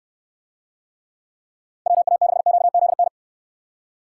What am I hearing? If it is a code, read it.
Morse code, heard as a keyed tone, about 62 wpm, 700 Hz. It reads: WA6QZN